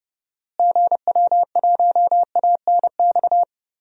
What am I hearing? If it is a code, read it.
Morse code, heard as a keyed tone, about 30 wpm, 700 Hz. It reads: GW1ANX
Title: DXCC-100 Call Signs - 6 Characters - Repeated 3x - 30wpm